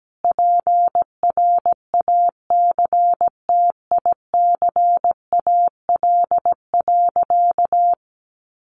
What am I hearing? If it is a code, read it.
Morse code, heard as a keyed tone, about 17 wpm, 700 Hz. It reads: PRACTICAL.
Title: Sentences from Top 1400 Words 17wpm